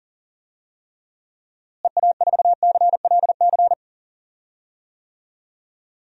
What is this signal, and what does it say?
Morse code, heard as a keyed tone, about 40 wpm, 700 Hz. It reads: EA4CLC
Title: DXCC-100 Call Signs - 6 Characters - Repeated 2x - 40wpm